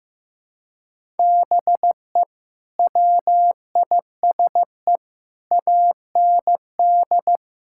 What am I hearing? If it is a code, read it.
Morse code, heard as a keyed tone, about 15 wpm, 700 Hz. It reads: BE WISE AND
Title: Sentences from Top 1600 Words 15wpm